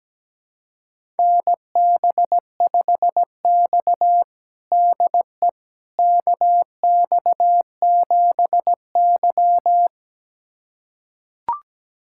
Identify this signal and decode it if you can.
Morse code, heard as a keyed tone, about 17 wpm, 700 Hz. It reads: NB5X DE KX7Y
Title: USA Callsign DE Callsign - 4 Characters - 17wpm